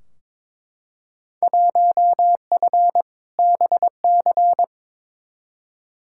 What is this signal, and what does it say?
Morse code, heard as a keyed tone, about 22 wpm, 700 Hz. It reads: 1F BC
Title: Field Day Exchanges Encore - 22wpm